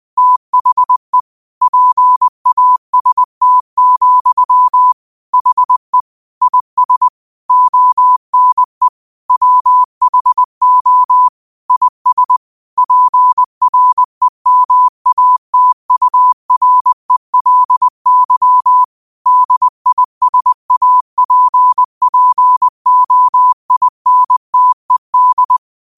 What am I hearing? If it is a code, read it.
Morse code, heard as a keyed tone, about 20 wpm, 1000 Hz. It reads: THE PAST, HE IS ONE WHO IS PREMATURELY DISAPPOINTED